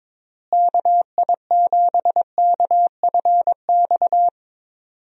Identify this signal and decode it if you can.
Morse code, heard as a keyed tone, about 21 wpm, 700 Hz. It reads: KI7KFX
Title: USA Call Signs - 6 Characters - Repeated 3x - 22wpm